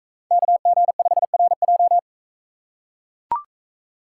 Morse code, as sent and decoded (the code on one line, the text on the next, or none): -.- --. ..... .-. .---
KG5RJ